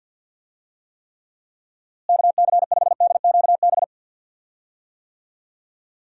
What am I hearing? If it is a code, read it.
Morse code, heard as a keyed tone, about 51 wpm, 700 Hz. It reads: KC5DQB
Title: USA Call Signs - 6 Characters - Repeated 3x - 50wpm